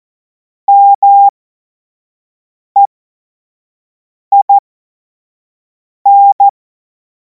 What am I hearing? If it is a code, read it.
Morse code, heard as a keyed tone, about 14 wpm, spaced as for 5 wpm, 800 Hz. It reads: MEIN
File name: Morsekurs / Witze / Witz 10. März 2026 25 F70